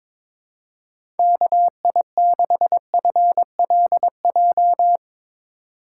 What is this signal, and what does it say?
Morse code, heard as a keyed tone, about 22 wpm, 700 Hz. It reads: KI6FLJ
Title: USA Call Signs - 6 Characters - 2x Character Spacing - 22wpm